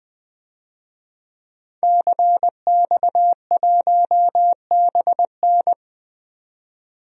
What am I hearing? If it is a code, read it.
Morse code, heard as a keyed tone, about 20 wpm, 700 Hz. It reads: CX1BN
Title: DXCC-100 Call Signs - 5 Characters - 20wpm